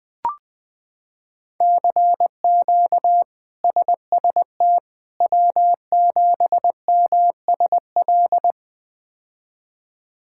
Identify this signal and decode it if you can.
Morse code, heard as a keyed tone, about 20 wpm, 700 Hz. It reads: CQ SST W7MSL
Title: K1USN SST QSO Practice - 20wpm